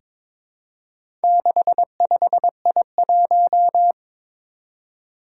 Morse code, heard as a keyed tone, about 22 wpm, 700 Hz. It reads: 65I1
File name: practice / 22wpm / Four Letter-Number Combinations - 22wpm